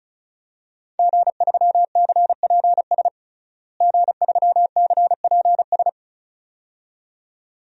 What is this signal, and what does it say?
Morse code, heard as a keyed tone, about 34 wpm, 700 Hz. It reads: G3CPS G3CPS
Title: DXCC-100 Call Signs - 5 Characters - Repeated 2x - 35wpm